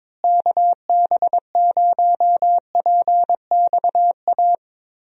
Morse code, heard as a keyed tone, about 22 wpm, 700 Hz. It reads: KB0PXA
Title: USA Call Signs - 6 Characters - Repeated 3x - 22wpm